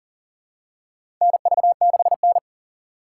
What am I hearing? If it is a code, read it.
Morse code, heard as a keyed tone, about 40 wpm, 700 Hz. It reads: NV6N
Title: USA Call Signs - 4 Characters - 6x Character Spacing - 40wpm